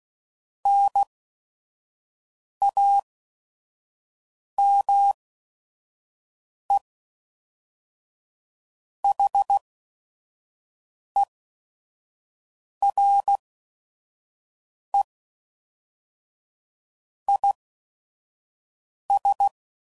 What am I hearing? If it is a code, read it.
Morse code, heard as a keyed tone, about 16 wpm, spaced as for 5 wpm, 800 Hz. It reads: NAME HERE IS
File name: CW Code Groups @5WPM / QSOs / WD0FBT 1